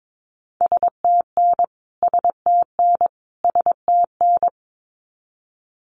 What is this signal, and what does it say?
Morse code, heard as a keyed tone, about 22 wpm, 700 Hz. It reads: STN STN STN
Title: Abbreviations Repeated 3x - 22wpm